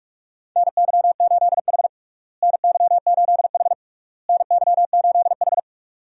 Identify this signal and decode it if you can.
Morse code, heard as a keyed tone, about 41 wpm, 700 Hz. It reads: NY8H NY8H NY8H
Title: USA Call Signs - 4 Characters - Repeated 3x - 45wpm